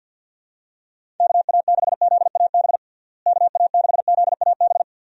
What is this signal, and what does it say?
Morse code, heard as a keyed tone, about 47 wpm, 700 Hz. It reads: KA6ZAB KA6ZAB
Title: USA Call Signs - 6 Characters - Repeated 2x - 50wpm